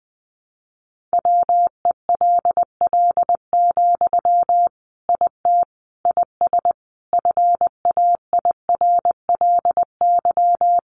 Morse code, heard as a keyed tone, about 20 wpm, 700 Hz. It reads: WELL, IT IS FAIRLY
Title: Sentences from Top 1100 Words 20wpm